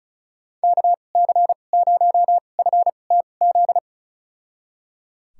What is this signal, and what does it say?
Morse code, heard as a keyed tone, about 34 wpm, spaced as for 25 wpm, 700 Hz. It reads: KC0FTZ
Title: USA Call Signs - 6 Characters - 2x Character Spacing - 35wpm